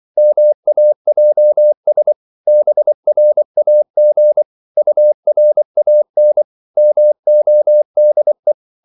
Morse code, heard as a keyed tone, about 24 wpm, 600 Hz. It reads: MAJS BRAG URAN MODE
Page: Korte danske ord 24wpm